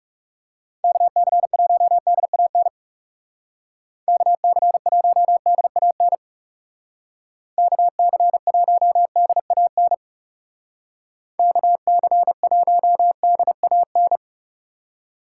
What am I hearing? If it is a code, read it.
Morse code, heard as a keyed tone, about 36 wpm, 700 Hz. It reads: KC1DAN KC1DAN KC1DAN KC1DAN
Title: Speed Racing - USA Call Signs 30wpm